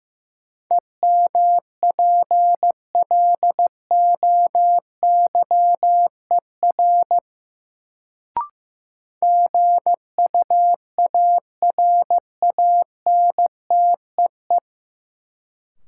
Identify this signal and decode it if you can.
Morse code, heard as a keyed tone, about 15 wpm, 700 Hz. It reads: EMPLOYER GUARANTEE
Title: Sentences from Top 1600 Words - New Word Drill - 15wpm